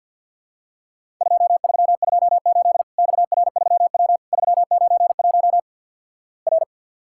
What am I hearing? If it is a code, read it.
Morse code, heard as a keyed tone, about 50 wpm, 700 Hz. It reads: 2328 XR3W 391 R
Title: Worked All Europe DX Contest - QTC Exchange - 50wpm